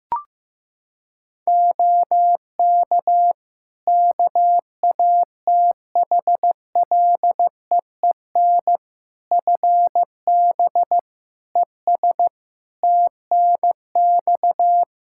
Morse code, heard as a keyed tone, about 15 wpm, 700 Hz. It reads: OK KATHLEEN FB ES TNX
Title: Practice QSO - Exchange 2 - Info - 15wpm